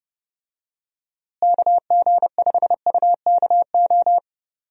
Morse code, heard as a keyed tone, about 30 wpm, 700 Hz. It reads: KG5UKO
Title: USA Call Signs - 6 Characters - Repeated 3x - 30wpm